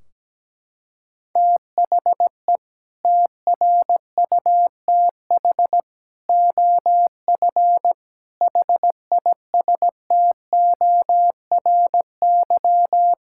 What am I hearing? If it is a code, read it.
Morse code, heard as a keyed tone, about 17 wpm, 700 Hz. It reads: THE TRUTH OF HISTORY
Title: Sentences from Top 1600 Words 17wpm